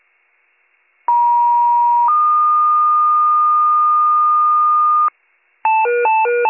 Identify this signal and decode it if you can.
Two-tone page, heard as a one-second tone, then a three-second tone, about 950 then 1220 Hz, with noise. It is heard from 1.1 to 5.1 s.